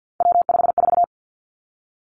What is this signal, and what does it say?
Morse code, heard as a keyed tone, about 50 wpm, 700 Hz. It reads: W54
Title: Three Challenging Letter-Number Combinations 50wpm